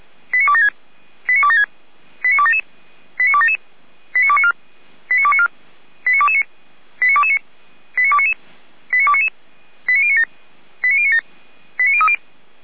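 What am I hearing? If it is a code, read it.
Five-tone selective call (ZVEI), heard as quick runs of five tones, tones of about 70 ms. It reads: 89276 89276 89277 89277 89294 89294 89229 89229 89299 89299 89097 89097 89030
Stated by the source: ZVEI five-tone selective call received with an SDR.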